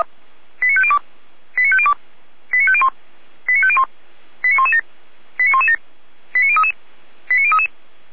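Five-tone selective call (ZVEI), heard as quick runs of five tones, tones of about 70 ms. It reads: 89502 89502 89501 89501 89117 89117 89033 89033
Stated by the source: ZVEI five-tone call received on SDR.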